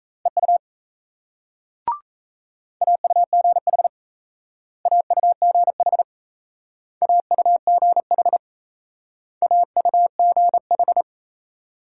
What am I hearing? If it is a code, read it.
Morse code, heard as a keyed tone, about 36 wpm, 700 Hz. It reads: EU AUGH AUGH AUGH AUGH